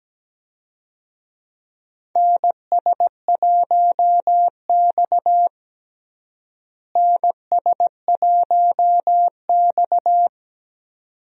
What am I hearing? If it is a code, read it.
Morse code, heard as a keyed tone, about 17 wpm, 700 Hz. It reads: NS1X NS1X
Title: USA Call Signs - 4 Characters - Repeated 2x - 17wpm